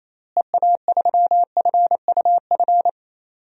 Morse code, heard as a keyed tone, about 27 wpm, 700 Hz. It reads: EA3FUF
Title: DXCC-100 Call Signs - 6 Characters - Repeated 3x - 28wpm